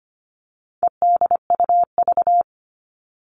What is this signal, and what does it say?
Morse code, heard as a keyed tone, about 25 wpm, 700 Hz. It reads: EDUV
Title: All Cut Numbers - 4-Digit 25wpm